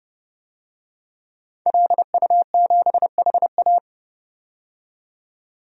Morse code, heard as a keyed tone, about 30 wpm, 700 Hz. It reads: LU7HA
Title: DXCC-100 Call Signs - 5 Characters - 30wpm